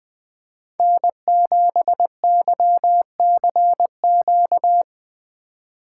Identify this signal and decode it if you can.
Morse code, heard as a keyed tone, about 19 wpm, 700 Hz. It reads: N7YCQ
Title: Speed Racing - USA 1x3 Call Signs 17wpm